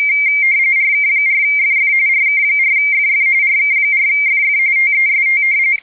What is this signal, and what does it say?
RTTY, about 45.45 baud. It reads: RK4FF RK4FF RK4FF TEST
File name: BDT-15 HAM